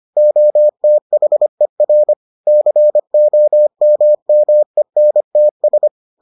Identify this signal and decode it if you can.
Morse code, heard as a keyed tone, about 25 wpm, 600 Hz. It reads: OTHER COMMENTS